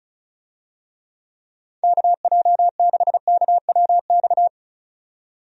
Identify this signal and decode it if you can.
Morse code, heard as a keyed tone, about 34 wpm, 700 Hz. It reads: KJ6KWX
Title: USA Call Signs - 6 Characters - 35wpm